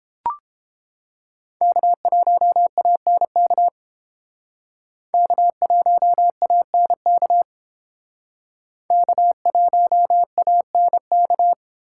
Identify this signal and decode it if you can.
Morse code, heard as a keyed tone, about 30 wpm, 700 Hz. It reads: K1ANK K1ANK K1ANK